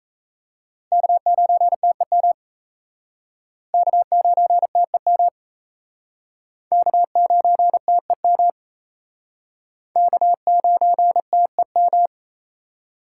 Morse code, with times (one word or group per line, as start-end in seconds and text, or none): 0.9-2.3 K9TEM
3.7-5.3 K9TEM
6.7-8.5 K9TEM
10.0-12.1 K9TEM